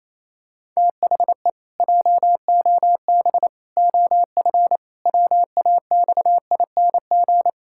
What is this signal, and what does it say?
Morse code, heard as a keyed tone, about 28 wpm, 700 Hz. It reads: THE JOB OF WAXING
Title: Pangram Challenge v2 - 28wpm